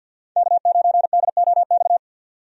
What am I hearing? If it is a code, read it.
Morse code, heard as a keyed tone, about 51 wpm, 700 Hz. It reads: K9DOX
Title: USA Call Signs - 5 Characters - 50wpm